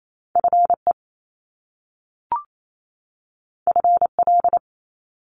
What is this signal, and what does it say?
Morse code, heard as a keyed tone, about 28 wpm, 700 Hz. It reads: FE FL
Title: Top Two Letter Combinations in Words 28wpm